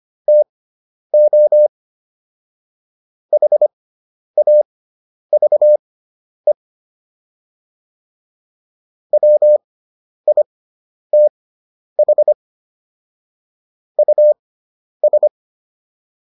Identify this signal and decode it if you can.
Morse code, heard as a keyed tone, about 25 wpm, spaced as for 10 wpm, 600 Hz. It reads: TO HAVE WITH US